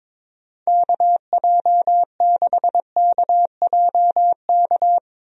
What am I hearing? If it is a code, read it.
Morse code, heard as a keyed tone, about 21 wpm, 700 Hz. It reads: KJ6KJK
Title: USA Call Signs - 6 Characters - 22wpm